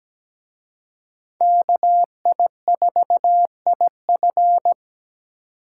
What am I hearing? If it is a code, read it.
Morse code, heard as a keyed tone, about 20 wpm, 700 Hz. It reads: KI4IF
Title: Speed Racing - USA 2x2 Call Signs 17wpm